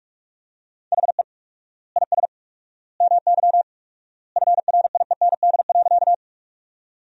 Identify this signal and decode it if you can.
Morse code, heard as a keyed tone, about 42 wpm, 700 Hz. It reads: HE IS MY FRIEND.
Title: 4x Word-Spacing - Sentences from Top 300 Words - 45wpm